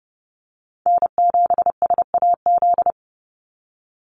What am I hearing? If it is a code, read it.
Morse code, heard as a keyed tone, about 30 wpm, 700 Hz. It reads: N7SAZ
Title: USA Call Signs 30wpm